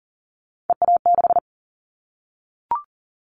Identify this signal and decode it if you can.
Morse code, heard as a keyed tone, about 42 wpm, 700 Hz. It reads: EA6
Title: Callsign Prefixes for Top 50 DXCC - 40wpm